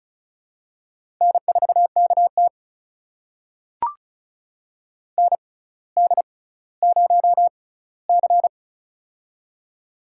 Morse code, heard as a keyed tone, about 34 wpm, spaced as for 18 wpm, 700 Hz. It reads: N4KT ND0C
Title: USA Call Signs - 4 Characters - 6x Character Spacing - 35wpm